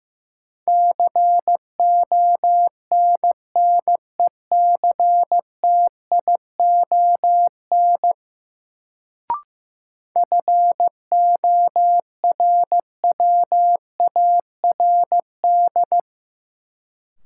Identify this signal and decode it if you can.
Morse code, heard as a keyed tone, about 15 wpm, 700 Hz. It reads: CONNECTION FORWARD
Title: Sentences from Top 1300 Words - New Word Drill Alt - 15wpm